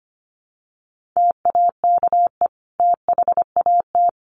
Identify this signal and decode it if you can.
Morse code, heard as a keyed tone, about 25 wpm, 700 Hz. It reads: TAKE THAT
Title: Sentences from Top 1300 Words 25wpm